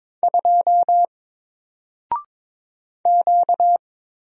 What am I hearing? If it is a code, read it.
Morse code, heard as a keyed tone, about 22 wpm, 700 Hz. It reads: 2 Q
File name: Single-Letter-Number-22wpm